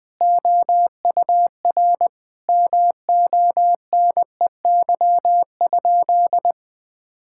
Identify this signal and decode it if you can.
Morse code, heard as a keyed tone, about 20 wpm, 700 Hz. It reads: OUR MONEY?
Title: Sentences from Top 1000 Words - Review Part 5 - 20wpm